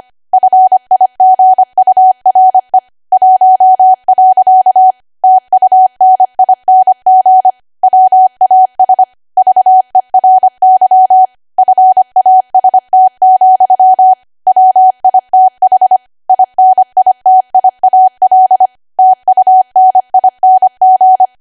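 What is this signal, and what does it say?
Morse code, heard as a keyed tone, about 25 wpm, 750 Hz. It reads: FIGURE 1. TUNING WAS VERY FAST, WITH INITIAL TUNING
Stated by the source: ARRL 25 WPM code practice archive.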